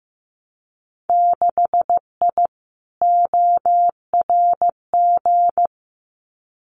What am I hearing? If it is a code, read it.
Morse code, heard as a keyed tone, about 15 wpm, 700 Hz. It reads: 6I ORG
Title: Winter Field Day Exchanges Encore 15wpm